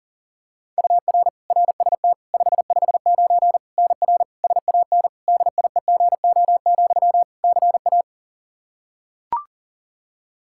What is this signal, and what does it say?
Morse code, heard as a keyed tone, about 37 wpm, 700 Hz. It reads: UR RST 559 NR SAN DIEGO, CA
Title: Practice QSO - Exchange 1 - Beginning - 40wpm